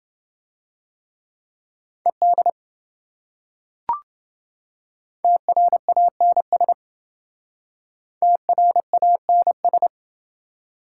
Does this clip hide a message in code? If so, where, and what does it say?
Yes: Morse code from 2.1 to 9.9 s, heard as a keyed tone, about 28 wpm, 700 Hz. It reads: ED TRANS TRANS